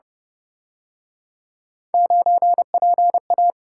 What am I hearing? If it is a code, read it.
Morse code, heard as a keyed tone, about 30 wpm, 700 Hz. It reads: 9PA
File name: Speed-Racing - Three Letter-Number Combinations - 25wpm